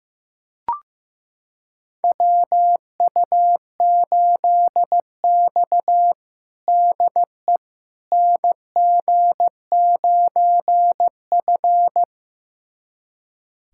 Morse code, heard as a keyed tone, about 15 wpm, 700 Hz. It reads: WU8X DE NG9F
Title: USA Callsign DE Callsign - 4 Characters - 15wpm